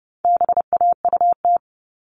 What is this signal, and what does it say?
Morse code, heard as a keyed tone, about 30 wpm, 700 Hz. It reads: BAUT
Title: All Cut Numbers - 4-Digit 30wpm